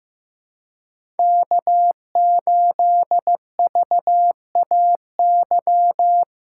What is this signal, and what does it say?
Morse code, heard as a keyed tone, about 15 wpm, 700 Hz. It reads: K8VAY
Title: USA Call Signs - 3x Character Spacing - 15wpm